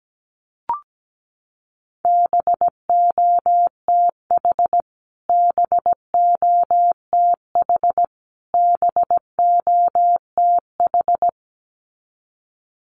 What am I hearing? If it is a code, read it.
Morse code, heard as a keyed tone, about 17 wpm, 700 Hz. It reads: BOTH BOTH BOTH
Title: Top 500 Words Repeated 17wpm